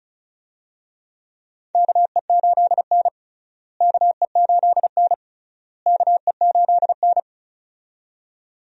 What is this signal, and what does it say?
Morse code, heard as a keyed tone, about 36 wpm, 700 Hz. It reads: KE8N KE8N KE8N
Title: USA Call Signs - 4 Characters - Repeated 3x - 35wpm